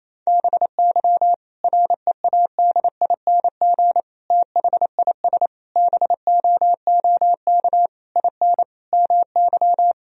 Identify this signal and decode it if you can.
Morse code, heard as a keyed tone, about 27 wpm, 700 Hz. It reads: BY READING THIS BOOK IN MY
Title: Sentences from Top 1000 Words - Review Part 3 - 28wpm